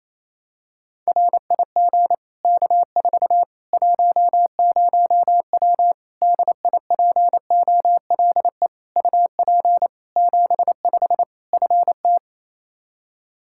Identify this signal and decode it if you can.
Morse code, heard as a keyed tone, about 28 wpm, 700 Hz. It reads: RIG K4 10W DIPOLE UP 75 FT
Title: Practice QSO - Exchange 2 - Info - 28wpm